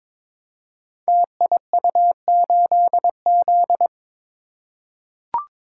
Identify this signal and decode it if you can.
Morse code, heard as a keyed tone, about 22 wpm, 700 Hz. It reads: TIU8Z
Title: Code Groups - 5 Letter:Numbers - 22wpm